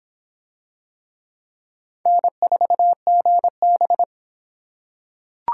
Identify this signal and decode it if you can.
Morse code, heard as a keyed tone, about 25 wpm, 700 Hz. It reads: N4GB